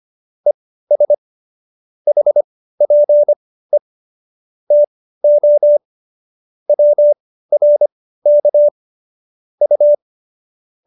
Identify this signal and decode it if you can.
Morse code, heard as a keyed tone, about 25 wpm, spaced as for 15 wpm, 590 Hz. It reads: ES HPE TO WRK U